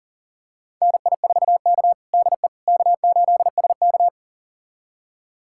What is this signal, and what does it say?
Morse code, heard as a keyed tone, about 40 wpm, 690 Hz. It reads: NI4K DE K8SK
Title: USA Callsign DE Callsign - 4 Characters - 40wpm